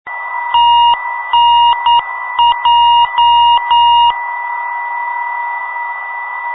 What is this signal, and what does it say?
Morse code, heard as a keyed tone, about 9 wpm, 2850 Hz, with noise. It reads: TNJ
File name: ndb filter testing / gen 318khz 400 24 8hz filter